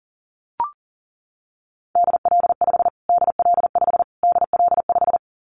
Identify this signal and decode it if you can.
Morse code, heard as a keyed tone, about 40 wpm, 700 Hz. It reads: DL5 DL5 DL5
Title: Callsign Prefixes for Top 25 DXCC - Repeated 3x - 40wpm